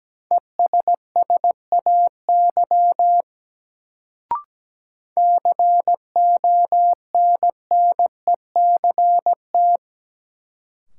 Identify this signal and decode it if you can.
Morse code, heard as a keyed tone, about 17 wpm, 700 Hz. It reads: ESSAY CONNECT
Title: Sentences from Top 1500 Words - New Word Drill Alt - 17wpm